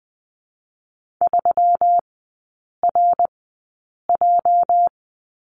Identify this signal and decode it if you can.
Morse code, heard as a keyed tone, about 20 wpm, spaced as for 8 wpm, 700 Hz. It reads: 3RJ
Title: 205 Lesson 33e - Letter j - Three Characters 20wpm